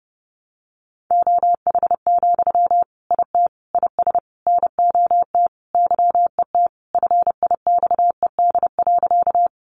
Morse code, heard as a keyed tone, about 30 wpm, 700 Hz. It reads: OH, IT IS NOT YET FIXED.
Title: Sentences from Top 1100 Words 30wpm